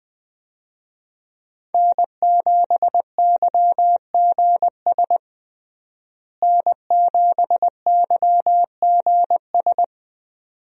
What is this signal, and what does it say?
Morse code, heard as a keyed tone, about 21 wpm, 700 Hz. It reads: N7YGS N7YGS
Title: USA Call Signs - 5 Characters - Repeated 2x - 20wpm